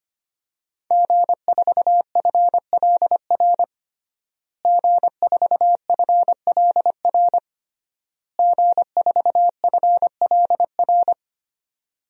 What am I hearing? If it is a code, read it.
Morse code, heard as a keyed tone, about 25 wpm, 700 Hz. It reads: G4FLR G4FLR G4FLR
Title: DXCC-100 Call Signs - 5 Characters - Repeated 3x - 25wpm